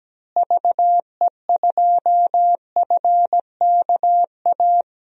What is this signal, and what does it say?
Morse code, heard as a keyed tone, about 17 wpm, 700 Hz. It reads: VE2FKA
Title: DXCC-100 Call Signs - 6 Characters - Repeated 3x - 17wpm